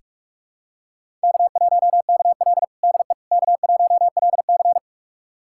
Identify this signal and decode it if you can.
Morse code, heard as a keyed tone, about 41 wpm, 700 Hz. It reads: K1KL DE K1LC
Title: USA Callsign DE Callsign - 4 Characters - 45wpm